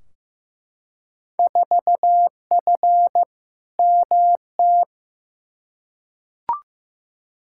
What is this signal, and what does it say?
Morse code, heard as a keyed tone, about 15 wpm, 700 Hz. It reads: 4F MT
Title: Field Day Exchanges Encore - 15wpm